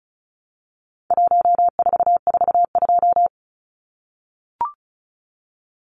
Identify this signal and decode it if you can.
Morse code, heard as a keyed tone, about 34 wpm, 700 Hz. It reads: 1442
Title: Numbers - 4-Digit 35wpm